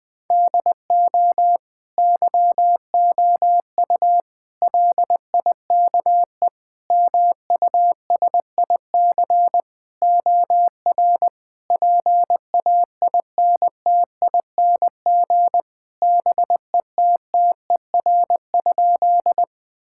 Morse code, heard as a keyed tone, about 20 wpm, 700 Hz. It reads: DO YOU LIKE MUSIC OR PAINTING BETTER?
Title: Sentences from Top 1000 Words - Review Part 6 - 20wpm